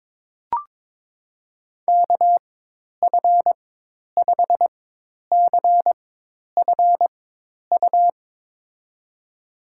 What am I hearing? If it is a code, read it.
Morse code, heard as a keyed tone, about 21 wpm, spaced as for 10 wpm, 700 Hz. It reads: KF5CFU
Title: USA Call Signs - 6 Characters - 4x Character Spacing - 22wpm